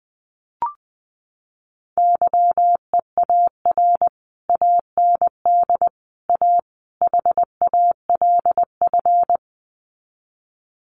Morse code, heard as a keyed tone, about 19 wpm, 700 Hz. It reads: YEAR AND A HALF
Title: Sets of 4 Words 20wpm